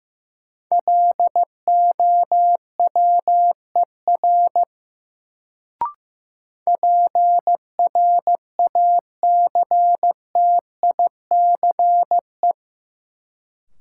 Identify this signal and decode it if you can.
Morse code, heard as a keyed tone, about 15 wpm, 700 Hz. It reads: LOWER PRACTICE
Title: Sentences from Top 700 Words - New Word Drill - 15wpm